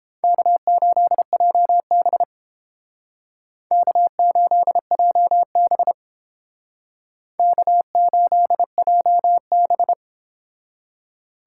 Morse code, heard as a keyed tone, about 28 wpm, 700 Hz. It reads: K8JB K8JB K8JB